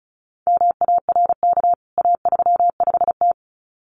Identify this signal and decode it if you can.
Morse code, heard as a keyed tone, about 34 wpm, 700 Hz. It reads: MARK A35T
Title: CWT Exchange 35wpm